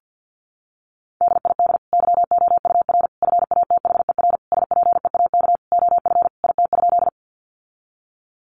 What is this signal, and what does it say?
Morse code, heard as a keyed tone, about 50 wpm, 700 Hz. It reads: DID YOUR FATHER SPEAK OF IT?
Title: Sentences from Top 1100 Words 50wpm